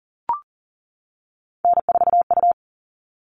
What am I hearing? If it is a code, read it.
Morse code, heard as a keyed tone, about 40 wpm, 700 Hz. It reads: N4U
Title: All Cut Numbers - 3-Digit 40wpm